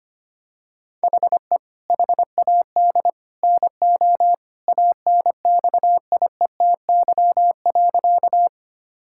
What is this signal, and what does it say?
Morse code, heard as a keyed tone, about 25 wpm, 700 Hz. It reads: HE HAD NO ANXIETY.
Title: Sentences from Top 1600 Words 25wpm